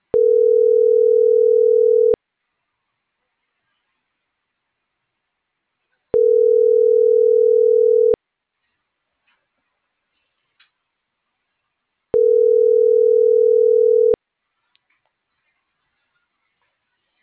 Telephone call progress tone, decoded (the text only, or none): ringback tone